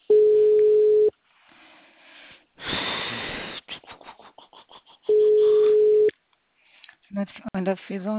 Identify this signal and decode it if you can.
Ringback tone (Europe, 425 Hz), with speech.